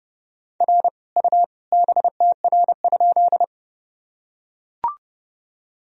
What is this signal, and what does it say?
Morse code, heard as a keyed tone, about 30 wpm, 700 Hz. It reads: R U BTR?
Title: Sentences from Top 400 Words and Abbreviations 30wpm